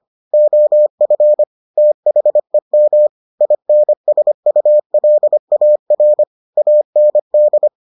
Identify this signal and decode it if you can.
Morse code, heard as a keyed tone, about 25 wpm, 600 Hz. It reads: OF THEM INSULAR AND